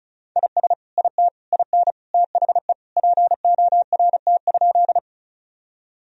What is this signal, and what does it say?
Morse code, heard as a keyed tone, about 36 wpm, 700 Hz. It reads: IS IT IN THE PORT?
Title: Sentences from Top 1000 Words - Review Part 1 - 35wpm